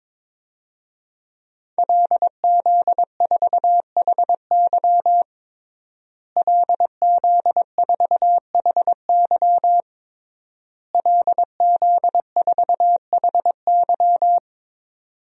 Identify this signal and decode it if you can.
Morse code, heard as a keyed tone, about 22 wpm, 700 Hz. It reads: LZ4HY LZ4HY LZ4HY
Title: DXCC-100 Call Signs - 5 Characters - Repeated 3x - 22wpm